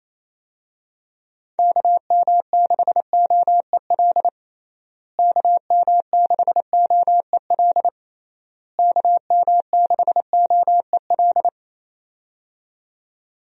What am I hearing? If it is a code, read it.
Morse code, heard as a keyed tone, about 28 wpm, 700 Hz. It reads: KM6OEL KM6OEL KM6OEL
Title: USA Call Signs - 6 Characters - Repeated 3x - 28wpm